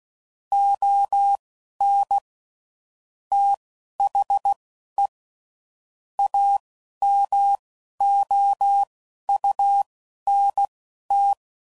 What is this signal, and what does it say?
Morse code, heard as a keyed tone, about 16 wpm, spaced as for 11 wpm, 800 Hz. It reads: ON THE AMOUNT
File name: CW Code Groups @10WPM / Technical Subjects / XC